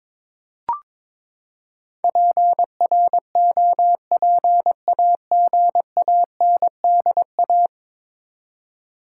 Morse code, heard as a keyed tone, about 21 wpm, 700 Hz. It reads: PROPAGANDA
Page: morse-code-ninja-practice-22wpm